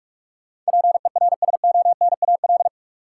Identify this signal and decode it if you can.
Morse code, heard as a keyed tone, about 46 wpm, 700 Hz. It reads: PERSONAL
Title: Sentences from Top 700 Words - New Word Drill - 45wpm